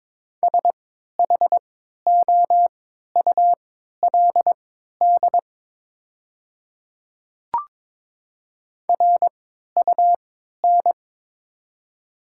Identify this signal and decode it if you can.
Morse code, heard as a keyed tone, about 22 wpm, spaced as for 12 wpm, 700 Hz. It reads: SHOULD RUN
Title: 3x Character Spacing - Top 300 Words - 22wpm